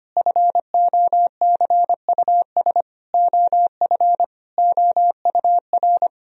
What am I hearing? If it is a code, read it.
Morse code, heard as a keyed tone, about 25 wpm, 700 Hz. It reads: FOCUS OF OUR
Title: Sentences from Top 1000 Words - Review Part 4 - 25wpm